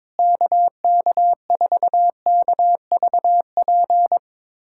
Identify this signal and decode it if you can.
Morse code, heard as a keyed tone, about 22 wpm, 700 Hz. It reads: KK4KVP
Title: K1USN SST QSO Practice - 22wpm